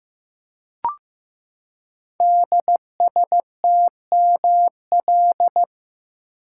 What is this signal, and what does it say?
Morse code, heard as a keyed tone, about 15 wpm, 700 Hz. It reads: DSTML
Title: Code Groups - 5 Letters - 15wpm